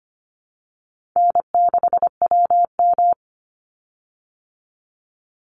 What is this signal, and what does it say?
Morse code, heard as a keyed tone, about 25 wpm, 700 Hz. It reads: N6WM
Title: Call Signs Repeated 3x - 25wpm